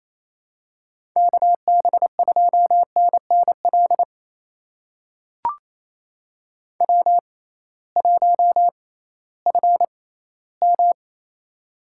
Morse code, heard as a keyed tone, about 28 wpm, spaced as for 16 wpm, 700 Hz. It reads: KB2NNL W1FM
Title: USA Call Signs - 6x Character Spacing - 28wpm